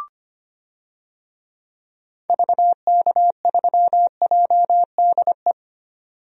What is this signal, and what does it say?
Morse code, heard as a keyed tone, about 25 wpm, 700 Hz. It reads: VK3JDE
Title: DXCC-100 Call Signs - 6 Characters - 25wpm